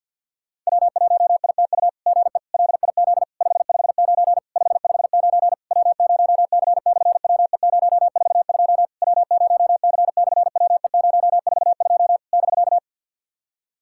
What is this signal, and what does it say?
Morse code, heard as a keyed tone, about 50 wpm, 700 Hz. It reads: W1ITU GE LIZ 559 559 W0C/WE042 W0C/WE042 <BK>
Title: SOTA QSO Practice 50wpm